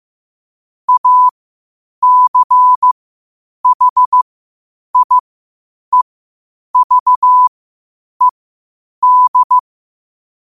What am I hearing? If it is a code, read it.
Morse code, heard as a keyed tone, about 15 wpm, spaced as for 8 wpm, 1000 Hz. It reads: ACHIEVED
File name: qotdfiles / 8wpm